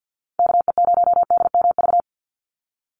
Morse code, heard as a keyed tone, about 50 wpm, 700 Hz. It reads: KE0DMV
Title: USA Call Signs 50wpm